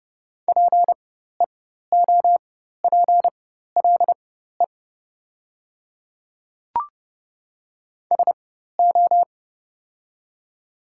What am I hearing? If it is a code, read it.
Morse code, heard as a keyed tone, about 30 wpm, spaced as for 14 wpm, 700 Hz. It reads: PEOPLE SO
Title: 4x Character Spacing - Top 100 Words - 30wpm